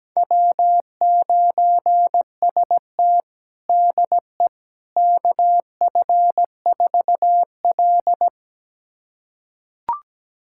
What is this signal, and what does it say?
Morse code, heard as a keyed tone, about 17 wpm, 700 Hz. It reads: W9ST DE KF4L
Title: USA Callsign DE Callsign - 4 Characters - 17wpm